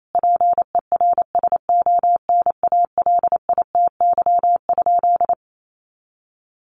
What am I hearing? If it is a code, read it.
Morse code, heard as a keyed tone, about 28 wpm, 700 Hz. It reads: PERSONALITY?
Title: Sentences from Top 1400 Words 28wpm